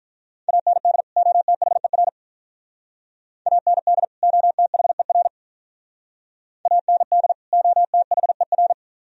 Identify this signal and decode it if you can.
Morse code, heard as a keyed tone, about 47 wpm, 700 Hz. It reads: AND OTHER AND OTHER AND OTHER